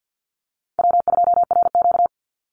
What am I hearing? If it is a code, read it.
Morse code, heard as a keyed tone, about 50 wpm, 700 Hz. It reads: W2RQ
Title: Easy Call Signs - 50wpm